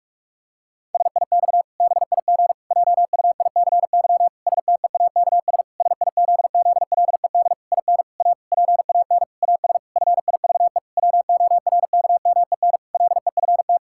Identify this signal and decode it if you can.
Morse code, heard as a keyed tone, about 41 wpm, 700 Hz. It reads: SIX BIG JUICY STEAKS SIZZLED IN A PAN AS FIVE WORKMEN LEFT
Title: Pangram Challenge v2 - 45wpm